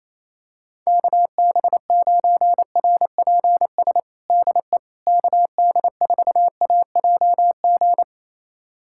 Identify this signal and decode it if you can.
Morse code, heard as a keyed tone, about 28 wpm, 700 Hz. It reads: KB9RPS DE KD4AJG
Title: USA Callsign DE Callsign - 6 Characters - 28wpm